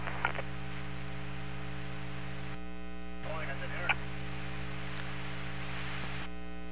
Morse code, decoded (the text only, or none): none